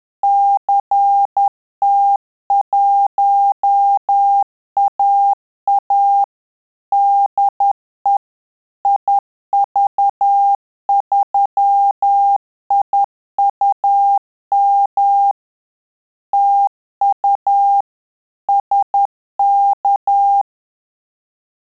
Morse code, heard as a keyed tone, about 10 wpm, 800 Hz. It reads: CT1AA DE IV3IUM TU SK
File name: Telegraphy-group / QSO CW 50 CPM